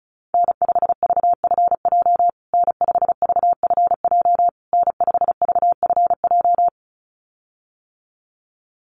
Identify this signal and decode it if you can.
Morse code, heard as a keyed tone, about 34 wpm, 700 Hz. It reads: N5VFJ N5VFJ N5VFJ
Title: USA Call Signs Repeated 3x 35wpm